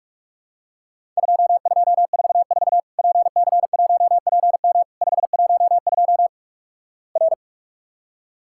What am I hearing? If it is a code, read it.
Morse code, heard as a keyed tone, about 46 wpm, 700 Hz. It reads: 2244 PC1PM 512 R
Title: Worked All Europe DX Contest - QTC Exchange - 45wpm